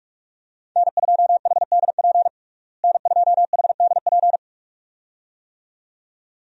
Morse code, heard as a keyed tone, about 41 wpm, 700 Hz. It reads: N2HDP N2HDP
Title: USA Call Signs - 5 Characters - Repeated 2x - 45wpm